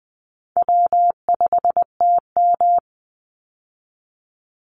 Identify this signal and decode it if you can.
Morse code, heard as a keyed tone, about 19 wpm, 700 Hz. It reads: W5TM
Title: Call Signs - 20wpm